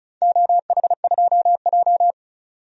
Morse code, heard as a keyed tone, about 36 wpm, 700 Hz. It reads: OH2J
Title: DXCC-100 Call Signs - 4 Characters - 35wpm